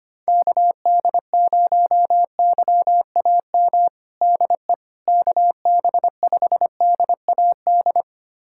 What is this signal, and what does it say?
Morse code, heard as a keyed tone, about 25 wpm, 700 Hz. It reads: KD0YAM DE KB5DAD
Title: USA Callsign DE Callsign - 6 Characters - 25wpm